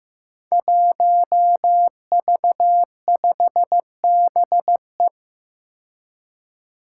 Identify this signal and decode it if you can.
Morse code, heard as a keyed tone, about 15 wpm, 700 Hz. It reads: 1V5BE
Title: Five Challenging Letter-Number Combinations 15wpm